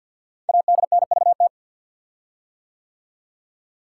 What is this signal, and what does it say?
Morse code, heard as a keyed tone, about 51 wpm, 700 Hz. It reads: ADNVT
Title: Code Groups - 5 Letters - 50wpm